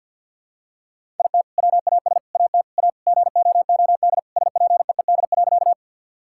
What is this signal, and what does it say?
Morse code, heard as a keyed tone, about 50 wpm, 700 Hz. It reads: IT WAS AT A GOOD SPEED.
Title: Sentences from Top 1000 Words - Review Part 5 - 50wpm